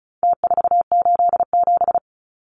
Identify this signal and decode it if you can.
Morse code, heard as a keyed tone, about 34 wpm, 700 Hz. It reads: T487
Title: Numbers - Contest Style - 4-Digit 35wpm